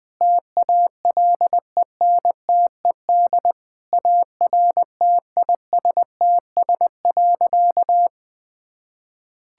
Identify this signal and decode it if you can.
Morse code, heard as a keyed tone, about 20 wpm, 700 Hz. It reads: TALENTED ARTISTS.
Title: Sentences with 8-Word Phrases - 20wpm